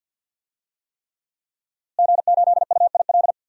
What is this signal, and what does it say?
Morse code, heard as a keyed tone, about 48 wpm, 700 Hz. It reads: G8UIL
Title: DXCC-100 Call Signs - 5 Characters - Repeated 3x - 50wpm